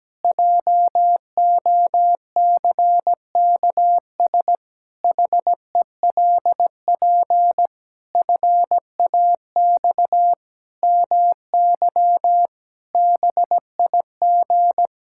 Morse code, heard as a keyed tone, about 17 wpm, 700 Hz. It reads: JOCKS HELP FAX MY BIG
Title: Pangram Challenge v2 - 17wpm